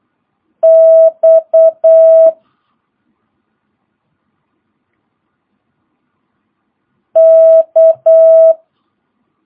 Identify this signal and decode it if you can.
Morse code, heard as a keyed tone, about 8 wpm, 650 Hz. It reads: X K
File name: jh8cbh / 2022cw